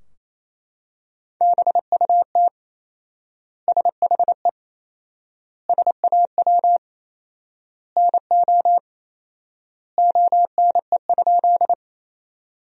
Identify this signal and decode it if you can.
Morse code, heard as a keyed tone, about 28 wpm, 700 Hz. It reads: BUT SHE SAW NO ONE?
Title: 4x Word-Spacing - Sentences from Top 300 Words - 28wpm